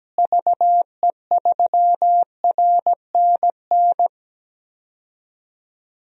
Morse code, heard as a keyed tone, about 17 wpm, 700 Hz. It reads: VE3RNN
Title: DXCC-100 Call Signs - 6 Characters - Repeated 2x - 17wpm